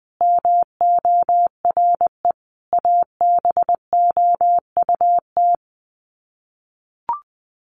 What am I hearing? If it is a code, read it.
Morse code, heard as a keyed tone, about 20 wpm, 700 Hz. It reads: MORE ABOUT
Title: Sets of 5 Words Encore2 20wpm